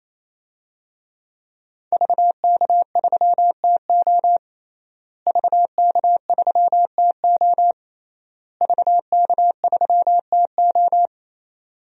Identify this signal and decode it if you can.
Morse code, heard as a keyed tone, about 27 wpm, 700 Hz. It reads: VK3TO VK3TO VK3TO
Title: DXCC-100 Call Signs - 5 Characters - Repeated 3x - 28wpm